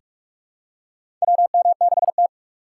Morse code, heard as a keyed tone, about 46 wpm, 700 Hz. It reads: WM6T
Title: USA Call Signs - 6x Character Spacing - 45wpm